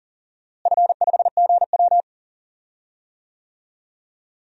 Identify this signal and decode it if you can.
Morse code, heard as a keyed tone, about 40 wpm, 690 Hz. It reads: F5GW
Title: DXCC-100 Call Signs - 4 Characters - Repeated 2x - 40wpm